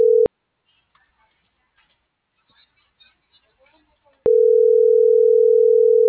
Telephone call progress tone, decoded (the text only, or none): ringback tone